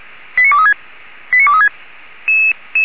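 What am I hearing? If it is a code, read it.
Five-tone selective call (ZVEI), heard as quick runs of five tones, tones of about 80 ms. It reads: 89236 89236
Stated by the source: ZVEI five-tone selective call 89242 received by SDR.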